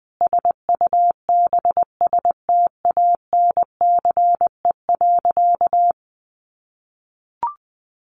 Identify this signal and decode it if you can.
Morse code, heard as a keyed tone, about 19 wpm, 700 Hz. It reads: SUBSTANCE.
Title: Sentences from Top 1500 Words 20wpm